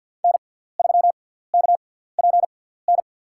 Morse code, heard as a keyed tone, about 53 wpm, spaced as for 17 wpm, 700 Hz. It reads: N3KPN
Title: USA Call Signs - 5 Characters - 6x Character Spacing - 50wpm